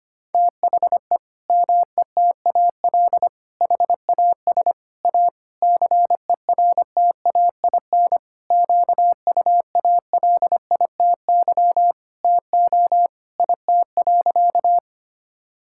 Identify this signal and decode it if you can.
Morse code, heard as a keyed tone, about 25 wpm, 700 Hz. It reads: THE METAL HAS A CERTAIN QUALITY TO IT.
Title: Sentences from Top 1000 Words - Review Part 6 - 25wpm